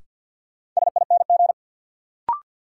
Morse code, heard as a keyed tone, about 52 wpm, 700 Hz. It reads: SING
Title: Sentences from Top 400 Words - New Word Drill - 50wpm